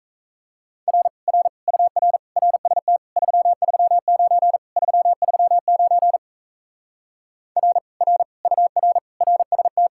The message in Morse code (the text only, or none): R R UR RST 339 339 R R UR RST